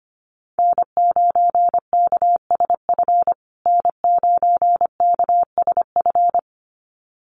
Morse code, heard as a keyed tone, about 25 wpm, 700 Hz. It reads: N9KSF N9KSF
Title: USA Call Signs Repeated 2x 25wpm